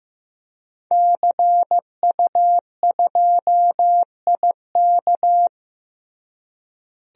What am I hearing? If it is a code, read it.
Morse code, heard as a keyed tone, about 16 wpm, 700 Hz. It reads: CU2IK
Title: DXCC-100 Call Signs - 5 Characters - Repeated 2x - 15wpm